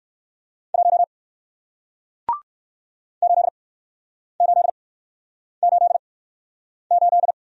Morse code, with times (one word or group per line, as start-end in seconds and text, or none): 0.7-1.0 1
3.2-7.3 8888